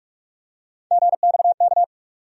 Morse code, heard as a keyed tone, about 42 wpm, 700 Hz. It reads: GXK
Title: USA Call Sign Postfixes - 3 Letters 45wpm